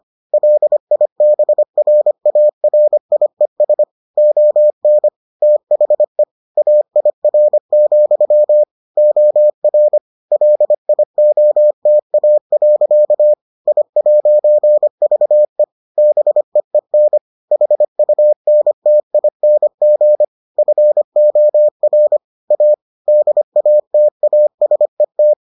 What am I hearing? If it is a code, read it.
Morse code, heard as a keyed tone, about 25 wpm, 600 Hz. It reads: LIBRARIES ON THE AIR, OR LIOTA. I'VE BEEN HUNTING FOR A DATASET